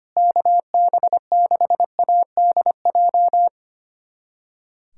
Morse code, heard as a keyed tone, about 25 wpm, 700 Hz. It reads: KB6ADJ
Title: USA Callsign DE Callsign 25wpm